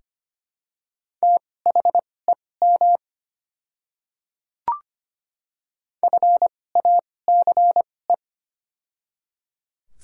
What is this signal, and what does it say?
Morse code, heard as a keyed tone, about 25 wpm, spaced as for 18 wpm, 700 Hz. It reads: THEM FACE
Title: 2x Character Spacing - Top 500 Words - 25wpm